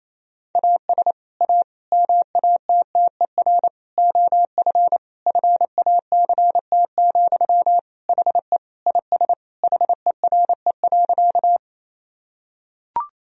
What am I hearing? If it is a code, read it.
Morse code, heard as a keyed tone, about 28 wpm, 700 Hz. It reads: AS A MATTER OF FACT, HE IS HERE.
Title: Sentences from Top 1000 Words - Review Part 3 - 28wpm